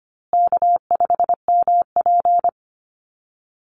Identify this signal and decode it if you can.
Morse code, heard as a keyed tone, about 25 wpm, 700 Hz. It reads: K5MP
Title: Easy Call Signs Repeated 3x - 25wpm